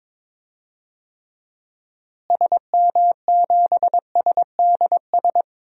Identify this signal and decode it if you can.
Morse code, heard as a keyed tone, about 22 wpm, 700 Hz. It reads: SM7SDS
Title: DXCC-100 Call Signs - 6 Characters - Repeated 3x - 22wpm